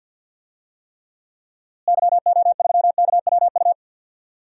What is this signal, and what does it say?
Morse code, heard as a keyed tone, about 47 wpm, 700 Hz. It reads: YO3KWU
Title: DXCC-100 Call Signs - 6 Characters - Repeated 2x - 50wpm